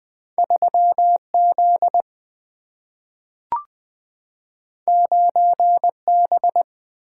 Morse code, heard as a keyed tone, about 20 wpm, 700 Hz. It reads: 3Z 9B
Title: Two Letter-Number Combinations - 20wpm